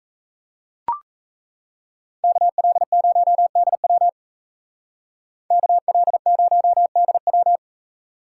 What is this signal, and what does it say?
Morse code, heard as a keyed tone, about 34 wpm, 700 Hz. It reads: KL0DW KL0DW
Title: Speed Racing - USA Call Signs 28wpm